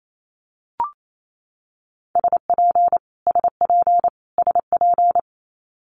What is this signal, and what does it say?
Morse code, heard as a keyed tone, about 28 wpm, 700 Hz. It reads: SP SP SP
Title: Multi-letter Phonemes Repeated 3x 28wpm